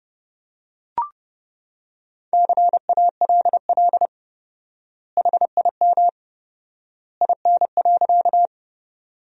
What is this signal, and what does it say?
Morse code, heard as a keyed tone, about 30 wpm, 700 Hz. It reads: CALL HIM IN.
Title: 4x Word-Spacing - Sentences from Top 100 Words - 30wpm